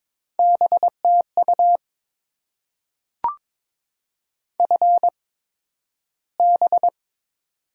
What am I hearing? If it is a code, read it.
Morse code, heard as a keyed tone, about 22 wpm, spaced as for 12 wpm, 700 Hz. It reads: BTU FB
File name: Top 20 QSO Elements - 8x Character Spacing - 22wpm